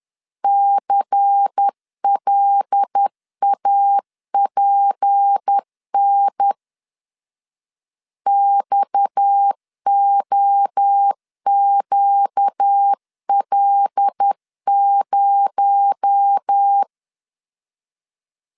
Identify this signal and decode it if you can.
Morse code, heard as a keyed tone, about 10 wpm, 800 Hz. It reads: CLAPN XOQL0